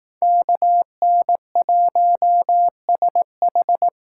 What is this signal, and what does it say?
Morse code, heard as a keyed tone, about 18 wpm, 700 Hz. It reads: KN1SH
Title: Speed Racing - USA 2x2 Call Signs 15wpm